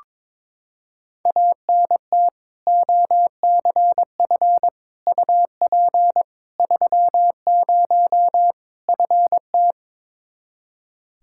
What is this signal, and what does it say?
Morse code, heard as a keyed tone, about 21 wpm, 700 Hz. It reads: ANT OCF UP 30 FT
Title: Practice QSO - Exchange 2 - Info - 22wpm